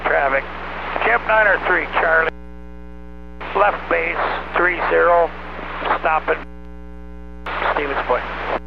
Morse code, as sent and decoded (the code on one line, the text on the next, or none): none